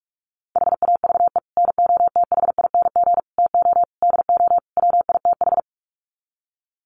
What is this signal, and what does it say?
Morse code, heard as a keyed tone, about 44 wpm, 700 Hz. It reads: HAVE NOTHING TO DO WITH
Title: Sets of 5 Words 45wpm